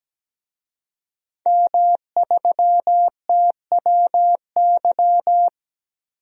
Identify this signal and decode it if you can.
Morse code, heard as a keyed tone, about 17 wpm, 700 Hz. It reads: M3TWY
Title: DXCC-100 Call Signs - 5 Characters - 17wpm